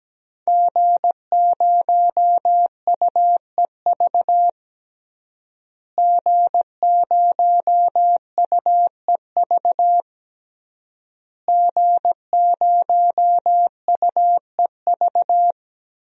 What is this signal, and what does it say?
Morse code, heard as a keyed tone, about 17 wpm, 700 Hz. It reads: G0UEV G0UEV G0UEV
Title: DXCC-100 Call Signs - 5 Characters - Repeated 3x - 17wpm